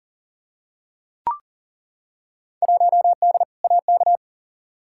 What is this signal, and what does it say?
Morse code, heard as a keyed tone, about 38 wpm, 700 Hz. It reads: 1D AK
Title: Field Day Exchanges - 40wpm